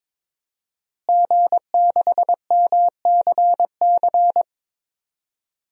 Morse code, heard as a keyed tone, about 22 wpm, 700 Hz. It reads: G6MCC
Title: DXCC-100 Call Signs - 5 Characters - Repeated 2x - 22wpm